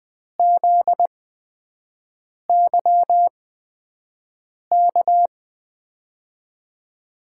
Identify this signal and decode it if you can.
Morse code, heard as a keyed tone, about 20 wpm, spaced as for 5 wpm, 700 Hz. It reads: ZYK
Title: USA Call Signs - 6 Characters - 8x Character Spacing - 20wpm